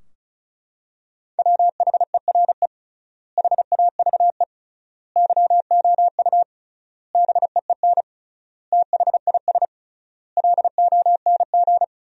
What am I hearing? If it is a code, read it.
Morse code, heard as a keyed tone, about 34 wpm, 700 Hz. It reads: WHERE HAVE YOU BEEN THIS LONG
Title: 3x Word-Spacing - Sentences from Top 300 Words - 35wpm